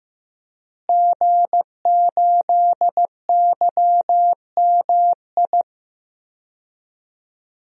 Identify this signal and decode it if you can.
Morse code, heard as a keyed tone, about 15 wpm, 700 Hz. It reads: G8YMI
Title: DXCC-100 Call Signs - 5 Characters - Repeated 3x - 15wpm